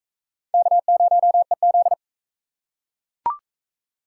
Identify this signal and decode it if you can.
Morse code, heard as a keyed tone, about 39 wpm, 700 Hz. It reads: K0EZ